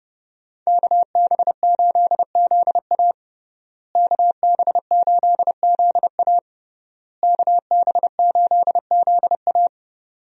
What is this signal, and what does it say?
Morse code, heard as a keyed tone, about 30 wpm, 700 Hz. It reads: KB8ZA KB8ZA KB8ZA
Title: USA Call Signs - 5 Characters - Repeated 3x - 30wpm